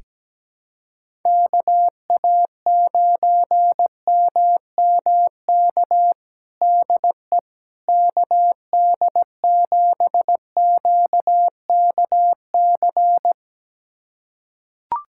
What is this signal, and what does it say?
Morse code, heard as a keyed tone, about 17 wpm, 700 Hz. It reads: KA9MMK DE KD7QKC
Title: USA Callsign DE Callsign 17wpm